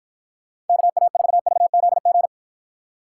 Morse code, heard as a keyed tone, about 53 wpm, 700 Hz. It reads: KA4VZG